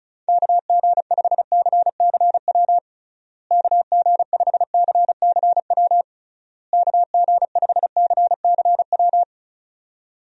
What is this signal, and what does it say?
Morse code, heard as a keyed tone, about 34 wpm, 700 Hz. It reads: KG5CCW KG5CCW KG5CCW
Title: USA Call Signs - 6 Characters - Repeated 3x - 35wpm